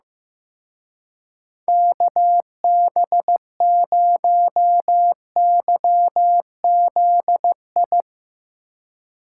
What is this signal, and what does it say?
Morse code, heard as a keyed tone, about 15 wpm, 700 Hz. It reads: KB0YZI
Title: Speed Racing - USA 2x3 Call Signs 15wpm